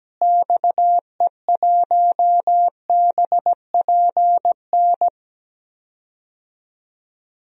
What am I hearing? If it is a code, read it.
Morse code, heard as a keyed tone, about 17 wpm, 700 Hz. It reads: XE1BPN
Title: DXCC-100 Call Signs - 6 Characters - Repeated 2x - 17wpm